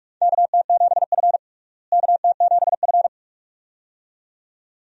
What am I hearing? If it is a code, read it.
Morse code, heard as a keyed tone, about 46 wpm, 700 Hz. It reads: KT7F KT7F
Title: USA Call Signs - 4 Characters - Repeated 2x - 45wpm